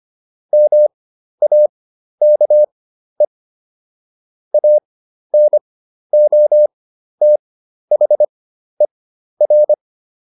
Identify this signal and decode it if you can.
Morse code, heard as a keyed tone, about 25 wpm, spaced as for 12 wpm, 600 Hz. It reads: MAKE ANOTHER